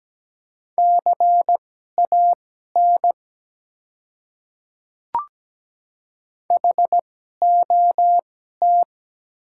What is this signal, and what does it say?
Morse code, heard as a keyed tone, about 17 wpm, spaced as for 12 wpm, 700 Hz. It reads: CAN HOT
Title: 2x Character Spacing - Top 100 Words - 17wpm